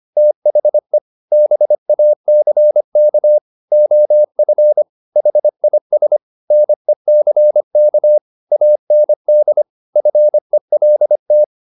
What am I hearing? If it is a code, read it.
Morse code, heard as a keyed tone, about 25 wpm, 600 Hz. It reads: THE BACK OF HIS NECK AND FELT
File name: the old man and the sea part 1 25wpm